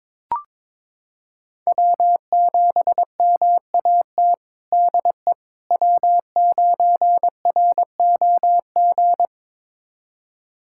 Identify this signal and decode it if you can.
Morse code, heard as a keyed tone, about 22 wpm, 700 Hz. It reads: W7MAT DE W9ROG
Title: USA Callsign DE Callsign - 5 Characters - 22wpm